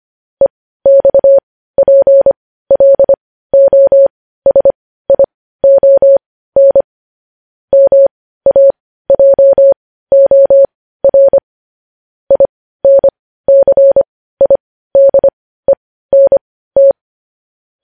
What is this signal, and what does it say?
Morse code, heard as a keyed tone, about 25 wpm, spaced as for 15 wpm, 550 Hz. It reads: EXPLOSION MAJOR INCIDENT
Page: News Headlines in Morse Code at 15 WPM